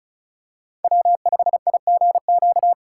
Morse code, heard as a keyed tone, about 36 wpm, 700 Hz. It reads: W5IGQ
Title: USA Call Signs - 5 Characters - 3x Character Spacing - 35wpm